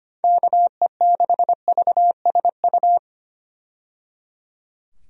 Morse code, heard as a keyed tone, about 25 wpm, 700 Hz. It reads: KE6VSU
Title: USA Call Signs - 6 Characters - 25wpm